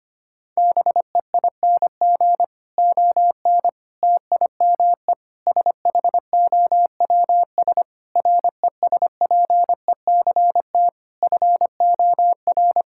Morse code, heard as a keyed tone, about 25 wpm, 700 Hz. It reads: BEING ON TIME SHOWS RESPECT FOR
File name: practice / 25wpm / Difficult Sentences with Hints - 25wpm